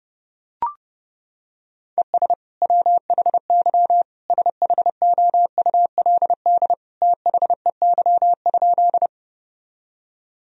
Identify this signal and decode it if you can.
Morse code, heard as a keyed tone, about 30 wpm, 700 Hz. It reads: ES WHY SHOULD THEY?
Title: Sentences from Top 300 Words and Abbreviations 30wpm